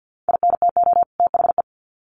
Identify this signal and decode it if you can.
Morse code, heard as a keyed tone, about 50 wpm, 700 Hz. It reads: INTO THE
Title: Sets of 2 Words 50wpm